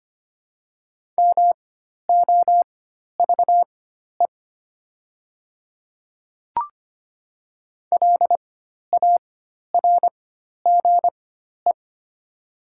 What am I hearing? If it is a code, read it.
Morse code, heard as a keyed tone, about 25 wpm, spaced as for 12 wpm, 700 Hz. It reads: MOVE LARGE
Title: 4x Character Spacing - Top 300 Words - 25wpm